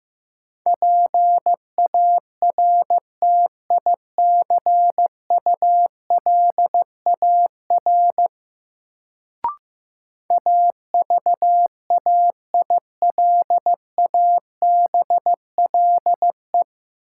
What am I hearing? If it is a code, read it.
Morse code, heard as a keyed tone, about 15 wpm, 700 Hz. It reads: PARTICULAR AVAILABLE
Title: Sentences from Top 600 Words - New Word Drill - 15wpm